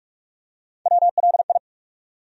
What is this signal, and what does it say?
Morse code, heard as a keyed tone, about 46 wpm, 700 Hz. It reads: WLI
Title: USA Call Sign Postfixes - 3 Letters 45wpm